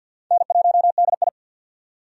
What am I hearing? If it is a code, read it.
Morse code, heard as a keyed tone, about 48 wpm, 700 Hz. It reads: N1DI
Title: USA Call Signs - 4 Characters - 50wpm